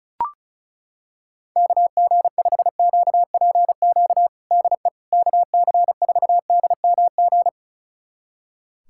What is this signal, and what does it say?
Morse code, heard as a keyed tone, about 36 wpm, 700 Hz. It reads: KG5QPQ DE KC4DMG
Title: USA Callsign DE Callsign - 6 Characters - 35wpm